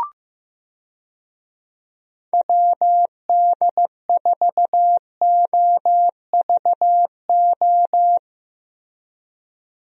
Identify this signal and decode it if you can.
Morse code, heard as a keyed tone, about 15 wpm, 700 Hz. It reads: WD4OVO
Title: USA Call Signs - 6 Characters - Repeated 2x - 15wpm